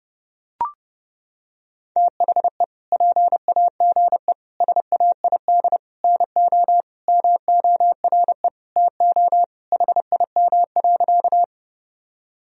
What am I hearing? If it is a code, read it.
Morse code, heard as a keyed tone, about 32 wpm, 700 Hz. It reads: THE PAGE SAID NO MORE TO HIM.
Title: Sentences from Top 1000 Words - Review Part 1 - 30wpm